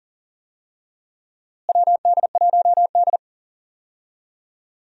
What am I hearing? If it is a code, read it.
Morse code, heard as a keyed tone, about 40 wpm, 700 Hz. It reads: WD1D
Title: USA Call Signs - 4 Characters - Repeated 3x - 40wpm